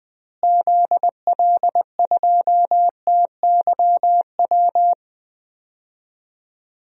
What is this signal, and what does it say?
Morse code, heard as a keyed tone, about 20 wpm, 700 Hz. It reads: ZL2TYW
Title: DXCC-100 Call Signs - 6 Characters - Repeated 2x - 20wpm